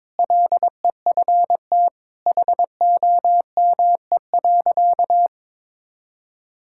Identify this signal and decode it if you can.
Morse code, heard as a keyed tone, about 22 wpm, 700 Hz. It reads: LEFT HOME.
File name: practice / 22wpm / Sentences with 4-Word Phrases - 22wpm